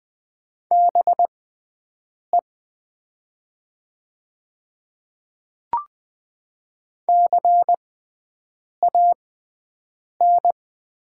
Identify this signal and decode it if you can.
Morse code, heard as a keyed tone, about 20 wpm, spaced as for 7 wpm, 700 Hz. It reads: BE CAN
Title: 6x Character Spacing - Top 100 Words - 20wpm